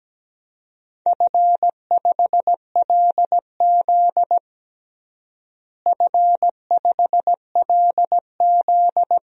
Morse code, heard as a keyed tone, about 17 wpm, 700 Hz. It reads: F5LZ F5LZ
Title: DXCC-100 Call Signs - 4 Characters - Repeated 2x - 17wpm